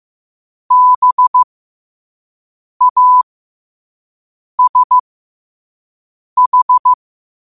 Morse code, heard as a keyed tone, about 15 wpm, spaced as for 5 wpm, 1000 Hz. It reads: BASH
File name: qotdfiles / 5wpm